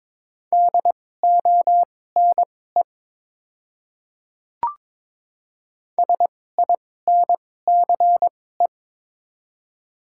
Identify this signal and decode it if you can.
Morse code, heard as a keyed tone, about 22 wpm, spaced as for 16 wpm, 700 Hz. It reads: DONE SINCE
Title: 2x Character Spacing - Top 500 Words - 22wpm